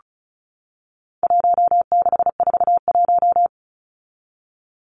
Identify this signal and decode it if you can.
Morse code, heard as a keyed tone, about 34 wpm, 700 Hz. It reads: 1641
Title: Numbers - Contest Style - 4-Digit 35wpm